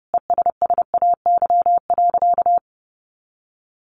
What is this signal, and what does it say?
Morse code, heard as a keyed tone, about 30 wpm, 700 Hz. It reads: ESSAY.
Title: Sentences from Top 1500 Words 30wpm